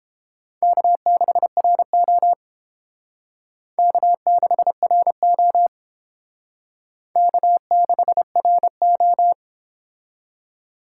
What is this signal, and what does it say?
Morse code, heard as a keyed tone, about 28 wpm, 700 Hz. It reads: K6RO K6RO K6RO